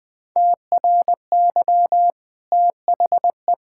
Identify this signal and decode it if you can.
Morse code, heard as a keyed tone, about 20 wpm, 700 Hz. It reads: TRY THE
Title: Easy Sentences with Hints - 20wpm